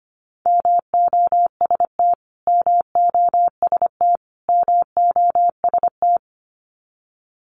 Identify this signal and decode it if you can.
Morse code, heard as a keyed tone, about 25 wpm, 700 Hz. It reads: MOST MOST MOST
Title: Top 300 Words Repeated 25wpm